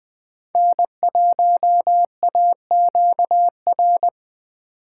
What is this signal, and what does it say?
Morse code, heard as a keyed tone, about 20 wpm, 700 Hz. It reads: N1AQR
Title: USA Call Signs - 5 Characters - 8x Character Spacing - 20wpm